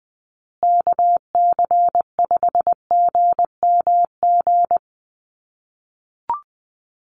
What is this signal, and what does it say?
Morse code, heard as a keyed tone, about 20 wpm, 700 Hz. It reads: KC5GMG
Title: USA Call Signs Repeated 3x 20wpm